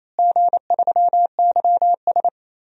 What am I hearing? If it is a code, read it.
Morse code, heard as a keyed tone, about 28 wpm, 700 Hz. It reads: G3YS
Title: DXCC-100 Call Signs - 4 Characters - Repeated 2x - 28wpm